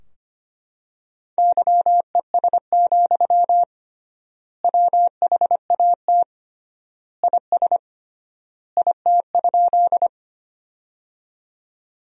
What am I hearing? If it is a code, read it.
Morse code, heard as a keyed tone, about 25 wpm, 700 Hz. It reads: YES, WHAT IS IT?
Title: 3x Word-Spacing - Sentences from Top 500 Words - 25wpm